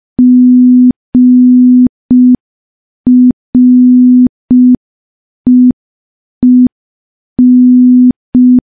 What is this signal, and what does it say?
Morse code, heard as a keyed tone, about 5 wpm, 250 Hz. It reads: GREEN
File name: morse / morse-training-05WPM